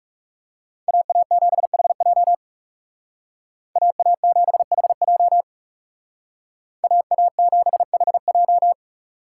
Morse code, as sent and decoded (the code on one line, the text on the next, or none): .- .- --... .... .--- / .- .- --... .... .--- / .- .- --... .... .---
AA7HJ AA7HJ AA7HJ